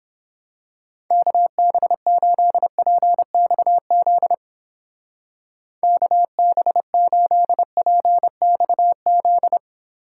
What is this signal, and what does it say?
Morse code, heard as a keyed tone, about 28 wpm, 700 Hz. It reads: KB8PXZ KB8PXZ